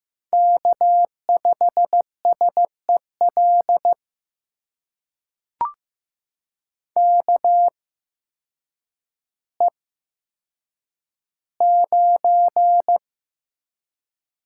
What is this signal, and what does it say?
Morse code, heard as a keyed tone, about 15 wpm, spaced as for 8 wpm, 700 Hz. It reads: K5SEL KE9
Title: USA Call Signs - 5 Characters - 8x Character Spacing - 15wpm